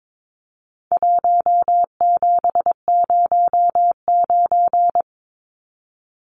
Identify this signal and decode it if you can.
Morse code, heard as a keyed tone, about 22 wpm, 700 Hz. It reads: 1709
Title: Numbers - Contest Style - 4-Digit 22wpm